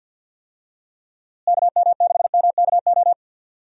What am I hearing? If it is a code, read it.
Morse code, heard as a keyed tone, about 52 wpm, 700 Hz. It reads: KM6MKO
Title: USA Call Signs - 6 Characters - Repeated 3x - 50wpm